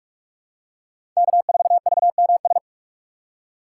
Morse code, heard as a keyed tone, about 43 wpm, 700 Hz. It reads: K4VMS
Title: USA Call Signs - 5 Characters - 3x Character Spacing - 45wpm